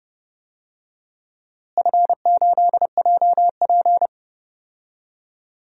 Morse code, heard as a keyed tone, about 30 wpm, 700 Hz. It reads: F8JP
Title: DXCC-100 Call Signs - 4 Characters - Repeated 2x - 30wpm